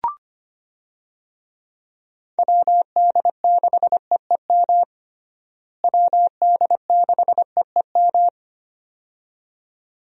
Morse code, heard as a keyed tone, about 25 wpm, 700 Hz. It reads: WD6EEM WD6EEM
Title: USA Call Signs - 6 Characters - Repeated 2x - 25wpm